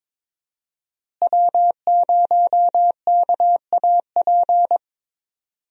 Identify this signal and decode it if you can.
Morse code, heard as a keyed tone, about 22 wpm, 700 Hz. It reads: W0KAP
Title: USA Call Signs - 5 Characters - Repeated 3x - 22wpm